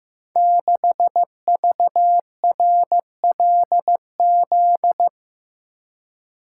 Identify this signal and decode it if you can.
Morse code, heard as a keyed tone, about 15 wpm, 700 Hz. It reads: 6VRLZ
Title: Code Groups - 5 Letter:Numbers - 15wpm